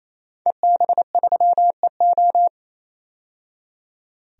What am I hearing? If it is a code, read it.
Morse code, heard as a keyed tone, about 28 wpm, 700 Hz. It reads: EB3EO
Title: DXCC-100 Call Signs - 5 Characters - 28wpm